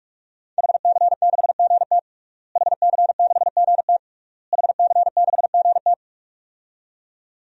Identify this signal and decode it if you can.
Morse code, heard as a keyed tone, about 41 wpm, 700 Hz. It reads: HC6GT HC6GT HC6GT
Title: DXCC-100 Call Signs - 5 Characters - Repeated 3x - 45wpm